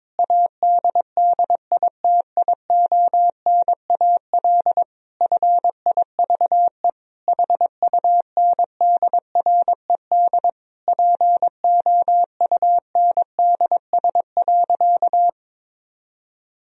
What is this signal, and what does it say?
Morse code, heard as a keyed tone, about 22 wpm, 700 Hz. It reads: ADDITIONAL FIVE HUNDRED POUNDS.
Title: Sentences from Top 1000 Words - Review Part 4 - 22wpm